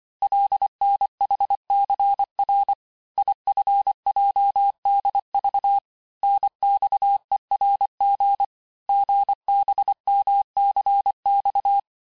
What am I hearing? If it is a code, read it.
Morse code, heard as a keyed tone, about 25 wpm, 800 Hz. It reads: LNHCR IFJDV NXERG GBMCX
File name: texte-chiffre-vitesse-23mpm